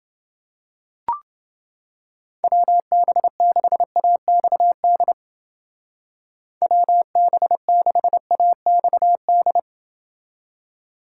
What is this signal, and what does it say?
Morse code, heard as a keyed tone, about 28 wpm, 700 Hz. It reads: WB6AXD WB6AXD